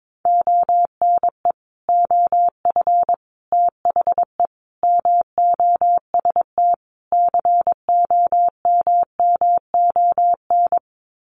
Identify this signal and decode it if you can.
Morse code, heard as a keyed tone, about 22 wpm, 700 Hz. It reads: ONE OF THE MOST COMMON
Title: Sets of 5 Words Encore2 22wpm